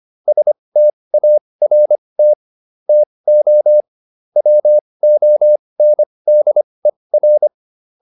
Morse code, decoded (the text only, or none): START TO WONDER